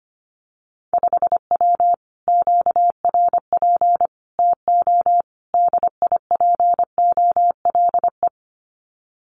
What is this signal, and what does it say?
Morse code, heard as a keyed tone, about 25 wpm, 700 Hz. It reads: 5W QRP TO DIPOLE
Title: Common QSO Phrases Encore 25wpm